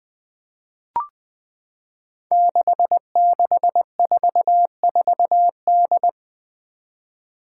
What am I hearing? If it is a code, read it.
Morse code, heard as a keyed tone, about 20 wpm, 700 Hz. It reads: 6644D
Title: Five Challenging Letter-Number Combinations 20wpm